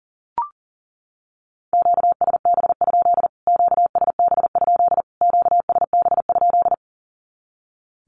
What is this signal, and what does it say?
Morse code, heard as a keyed tone, about 40 wpm, 700 Hz. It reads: QSB? QSB? QSB?
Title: Q-Codes Repeated 3x 40wpm